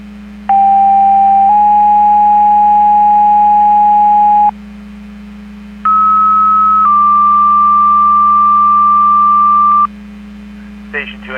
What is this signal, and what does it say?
Two-tone page, heard as a one-second tone, then a three-second tone, about 790 then 830 Hz, about 1290 then 1180 Hz, with speech. It is heard from 0.5 to 9.9 s.